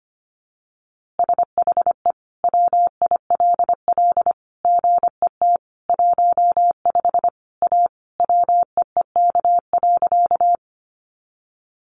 Morse code, heard as a keyed tone, about 25 wpm, 700 Hz. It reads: SHE WILL GET 15 A WEEK.
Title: Easy Phrases 25wpm